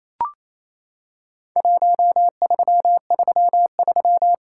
Morse code, heard as a keyed tone, about 28 wpm, 700 Hz. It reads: 1333
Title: Worked All Europe DX Contest - QTC Exchange - 28wpm